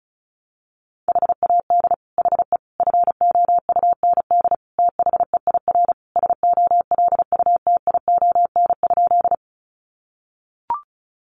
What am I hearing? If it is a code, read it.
Morse code, heard as a keyed tone, about 34 wpm, 700 Hz. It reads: HAD HE FOUND THEIR SOLUTION?
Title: Sentences from Top 1300 Words 35wpm